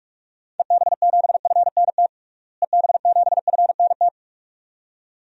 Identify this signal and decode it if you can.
Morse code, heard as a keyed tone, about 41 wpm, 700 Hz. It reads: EB7FNT EB7FNT
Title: DXCC-100 Call Signs - 6 Characters - Repeated 2x - 45wpm